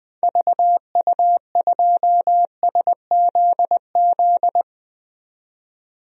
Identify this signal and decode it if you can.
Morse code, heard as a keyed tone, about 21 wpm, 700 Hz. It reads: VU2SZZ
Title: DXCC-100 Call Signs - 6 Characters - 20wpm